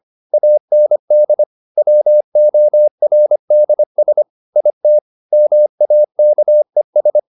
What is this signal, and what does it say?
Morse code, heard as a keyed tone, about 25 wpm, 600 Hz. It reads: AND WORDS IT MAKES